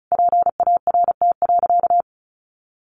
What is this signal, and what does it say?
Morse code, heard as a keyed tone, about 34 wpm, 700 Hz. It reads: PART.
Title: Sentences from Top 1200 Words 35wpm